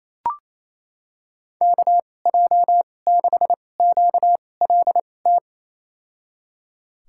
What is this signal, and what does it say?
Morse code, heard as a keyed tone, about 28 wpm, spaced as for 20 wpm, 700 Hz. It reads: KJ6QLT
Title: USA Call Signs - 6 Characters - 2x Character Spacing - 28wpm